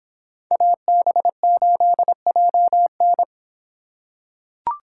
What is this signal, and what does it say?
Morse code, heard as a keyed tone, about 24 wpm, 700 Hz. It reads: AB8JN